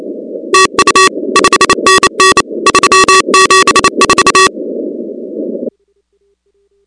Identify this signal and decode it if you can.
Morse code, heard as a keyed tone, about 28 wpm, 1140 Hz, with noise. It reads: TU 5NN 374
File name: contests / 2019-cw-wpx-kt5j-k5pi- -2330 0-37